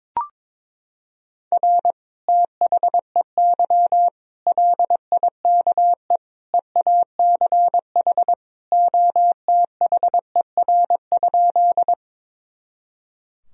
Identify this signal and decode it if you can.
Morse code, heard as a keyed tone, about 21 wpm, 700 Hz. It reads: R THEY LIKE EACH OTHER?
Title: Sentences from Top 200 Words and Abbreviations 22wpm